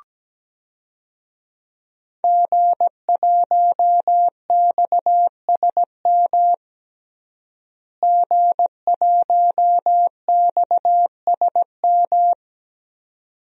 Morse code, heard as a keyed tone, about 17 wpm, 700 Hz. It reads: G1XSM G1XSM
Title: DXCC-100 Call Signs - 5 Characters - Repeated 2x - 17wpm